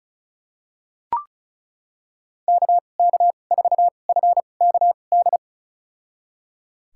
Morse code, heard as a keyed tone, about 34 wpm, spaced as for 25 wpm, 700 Hz. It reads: KK4FKD
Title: USA Call Signs - 2x Character Spacing - 35wpm